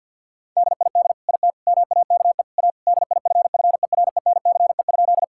Morse code, heard as a keyed tone, about 50 wpm, 700 Hz. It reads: DID IT MAKE A DIFFERENCE?
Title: Sentences from Top 1000 Words - Review Part 5 - 50wpm